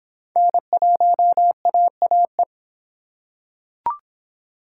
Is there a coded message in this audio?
Yes: Morse code from 0.4 to 2.4 s, heard as a keyed tone, about 24 wpm, 700 Hz. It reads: N1AAE